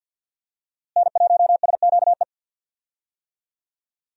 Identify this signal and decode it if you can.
Morse code, heard as a keyed tone, about 47 wpm, 700 Hz. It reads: N1SQE
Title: USA Call Signs - 5 Characters - Repeated 2x - 50wpm